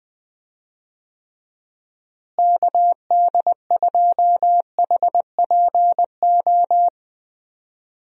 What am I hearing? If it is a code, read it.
Morse code, heard as a keyed tone, about 20 wpm, 700 Hz. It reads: KD2HPO
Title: USA Call Signs - 6 Characters - Repeated 2x - 20wpm